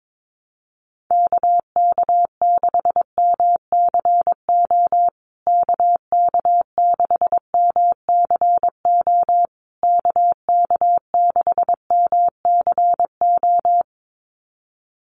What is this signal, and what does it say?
Morse code, heard as a keyed tone, about 22 wpm, 700 Hz. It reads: KK6MCO KK6MCO KK6MCO
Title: USA Call Signs Repeated 3x 22wpm